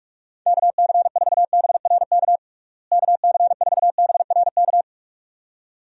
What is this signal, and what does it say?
Morse code, heard as a keyed tone, about 45 wpm, 700 Hz. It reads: KC4BRK KC4BRK
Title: USA Call Signs - 6 Characters - Repeated 2x - 45wpm